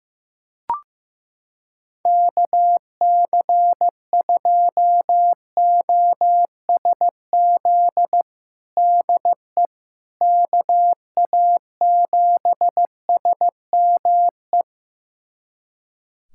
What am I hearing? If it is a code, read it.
Morse code, heard as a keyed tone, about 16 wpm, 700 Hz. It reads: KC2OSZ DE KA7SME
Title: USA Callsign DE Callsign - 6 Characters - 15wpm